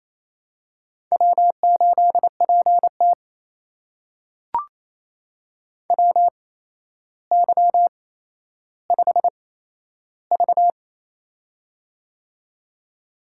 Morse code, heard as a keyed tone, about 27 wpm, spaced as for 12 wpm, 700 Hz. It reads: W8PT WY5V
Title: USA Call Signs - 4 Characters - 8x Character Spacing - 28wpm